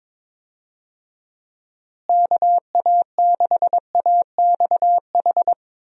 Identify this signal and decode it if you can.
Morse code, heard as a keyed tone, about 22 wpm, 700 Hz. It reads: KA6AXH
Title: USA Call Signs - 6 Characters - Repeated 3x - 22wpm